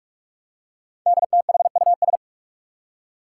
Morse code, heard as a keyed tone, about 44 wpm, 700 Hz. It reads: DTHUS
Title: Five Challenging Letter-Number Combinations 45wpm